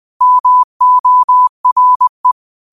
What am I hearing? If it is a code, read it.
Morse code, heard as a keyed tone, about 20 wpm, 1000 Hz. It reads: MORE